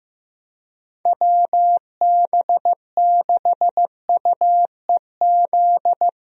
Morse code, heard as a keyed tone, about 15 wpm, 700 Hz. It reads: WB6UEZ
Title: USA Call Signs - 6 Characters - Repeated 2x - 15wpm